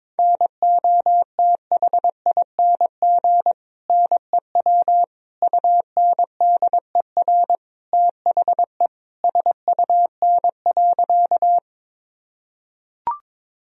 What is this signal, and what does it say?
Morse code, heard as a keyed tone, about 22 wpm, 700 Hz. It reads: NOTHING NEW UNDER THE SUN.
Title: Sentences from Top 1000 Words - Review Part 1 - 22wpm